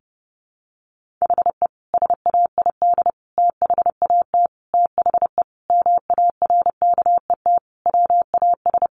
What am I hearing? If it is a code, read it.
Morse code, heard as a keyed tone, about 30 wpm, 700 Hz. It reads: HE SAID THAT THE MARKET WAS
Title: Sentences from Top 1400 Words 30wpm